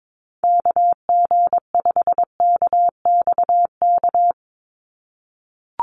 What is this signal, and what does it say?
Morse code, heard as a keyed tone, about 22 wpm, 700 Hz. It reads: KG5KXK
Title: USA Call Signs 22wpm